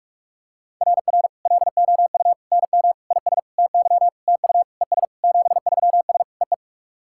Morse code, heard as a keyed tone, about 48 wpm, 700 Hz. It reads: RR LOU NM IS TY TU ES 73S EE
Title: SOTA QSO Practice 45wpm